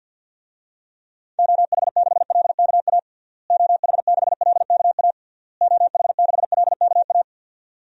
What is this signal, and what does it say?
Morse code, heard as a keyed tone, about 51 wpm, 700 Hz. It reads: OH6LKA OH6LKA OH6LKA
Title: DXCC-100 Call Signs - 6 Characters - Repeated 3x - 50wpm